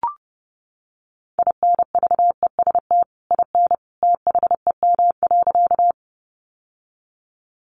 Morse code, heard as a keyed tone, about 30 wpm, 700 Hz. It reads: INVEST IN THEM.
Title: Sentences from Top 1100 Words 30wpm